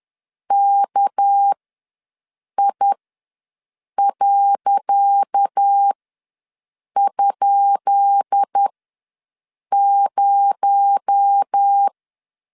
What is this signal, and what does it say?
Morse code, heard as a keyed tone, about 10 wpm, spaced as for 6 wpm, 800 Hz. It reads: KI.?0
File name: LEZIONI / 10 SIGNS / 10C DOT ws 10wpm 4213